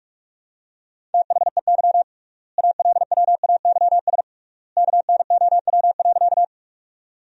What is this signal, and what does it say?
Morse code, heard as a keyed tone, about 41 wpm, 700 Hz. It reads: THEY ALWAYS KNOW.
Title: 3x Word-Spacing - Sentences from Top 300 Words - 45wpm